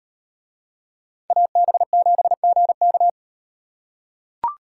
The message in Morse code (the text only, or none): AB7GK